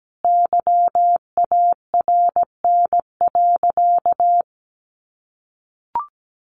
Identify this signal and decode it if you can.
Morse code, heard as a keyed tone, about 17 wpm, 700 Hz. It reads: YARN.
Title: Additional Easy Phrases 17wpm